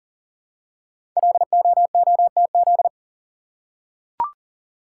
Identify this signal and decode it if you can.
Morse code, heard as a keyed tone, about 39 wpm, 690 Hz. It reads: LOOTZ
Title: Code Groups - 5 Letters - 40wpm